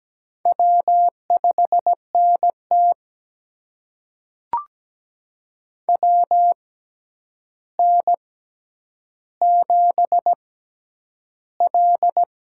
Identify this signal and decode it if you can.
Morse code, heard as a keyed tone, about 17 wpm, spaced as for 9 wpm, 700 Hz. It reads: W5NT WN7L
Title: USA Call Signs - 4 Characters - 6x Character Spacing - 17wpm